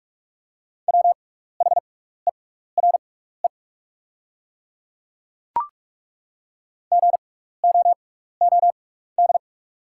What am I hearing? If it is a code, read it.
Morse code, heard as a keyed tone, about 43 wpm, spaced as for 15 wpm, 690 Hz. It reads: WHERE GOOD
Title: 6x Character Spacing - Top 200 Words - 45wpm